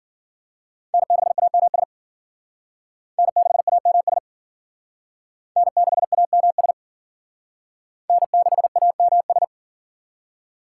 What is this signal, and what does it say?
Morse code, heard as a keyed tone, about 50 wpm, 700 Hz. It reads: N6AMS N6AMS N6AMS N6AMS